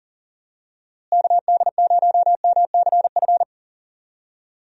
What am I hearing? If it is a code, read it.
Morse code, heard as a keyed tone, about 40 wpm, 700 Hz. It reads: KD0MCF
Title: USA Call Signs - 6 Characters - Repeated 2x - 40wpm